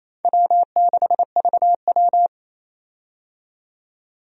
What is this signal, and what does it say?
Morse code, heard as a keyed tone, about 28 wpm, 700 Hz. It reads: W6VW
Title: USA Call Signs - 4 Characters - 28wpm